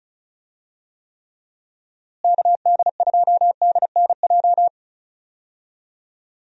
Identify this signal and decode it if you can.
Morse code, heard as a keyed tone, about 36 wpm, 700 Hz. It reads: KD2DNJ
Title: USA Call Signs - 6 Characters - Repeated 2x - 35wpm